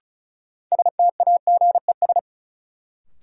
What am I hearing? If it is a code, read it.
Morse code, heard as a keyed tone, about 36 wpm, 700 Hz. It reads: STAGES
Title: Sets of 5 Words Encore - New Word Drill - 35wpm